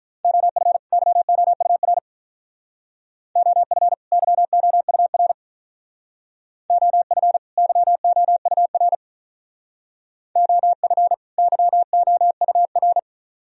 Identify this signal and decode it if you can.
Morse code, heard as a keyed tone, about 42 wpm, 700 Hz. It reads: OF YOUR OF YOUR OF YOUR OF YOUR
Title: Speed-Racing - Sets of 2 Words 35wpm